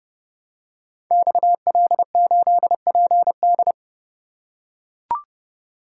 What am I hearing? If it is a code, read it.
Morse code, heard as a keyed tone, about 30 wpm, 700 Hz. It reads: XL8PD
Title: Code Groups - 5 Letter:Numbers - 30wpm